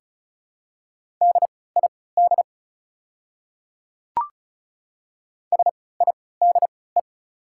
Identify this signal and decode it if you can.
Morse code, heard as a keyed tone, about 34 wpm, spaced as for 20 wpm, 690 Hz. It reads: DID SIDE
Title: 3x Character Spacing - Top 100 Words - 35wpm